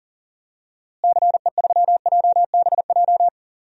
Morse code, heard as a keyed tone, about 40 wpm, 700 Hz. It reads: CE3JBJ
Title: DXCC-100 Call Signs - 6 Characters - Repeated 2x - 40wpm